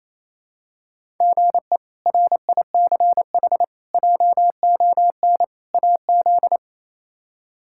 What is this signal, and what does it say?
Morse code, heard as a keyed tone, about 28 wpm, 700 Hz. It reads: GE RICH JON AZ
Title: K1USN SST QSO Practice - 28wpm